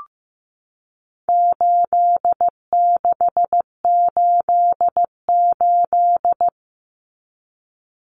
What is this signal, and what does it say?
Morse code, heard as a keyed tone, about 15 wpm, 700 Hz. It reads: 8688
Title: Numbers - 4-Digit 15wpm